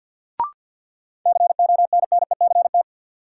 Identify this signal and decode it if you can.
Morse code, heard as a keyed tone, about 53 wpm, 700 Hz. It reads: CONNECT
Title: Sentences from Top 1500 Words - New Word Drill - 50wpm